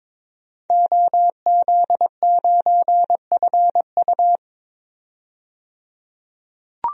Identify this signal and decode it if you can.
Morse code, heard as a keyed tone, about 22 wpm, 700 Hz. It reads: OZ9FU
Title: DXCC-100 Call Signs - 5 Characters - 22wpm